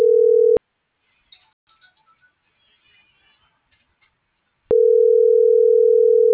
Ringback tone (North America), 440 + 480 Hz.